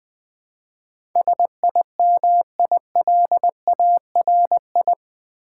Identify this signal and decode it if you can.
Morse code, heard as a keyed tone, about 20 wpm, 700 Hz. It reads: SIMILARI
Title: Build A Word - 4 Letters At A Time - 10-Letter Words - 20wpm